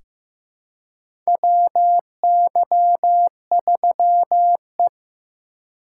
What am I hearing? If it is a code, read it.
Morse code, heard as a keyed tone, about 16 wpm, 700 Hz. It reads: WY3E
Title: USA Call Signs - 4 Characters - 2x Character Spacing - 15wpm